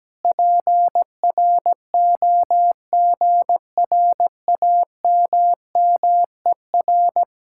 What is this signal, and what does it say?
Morse code, heard as a keyed tone, about 17 wpm, 700 Hz. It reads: PROGRAMMER
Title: Speed-Racing Common QSO Phrases - 17wpm